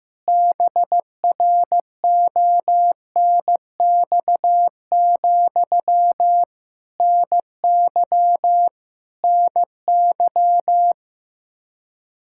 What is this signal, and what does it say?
Morse code, heard as a keyed tone, about 15 wpm, 700 Hz. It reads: BRONX, NY NY
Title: K1USN SST QSO Practice 15wpm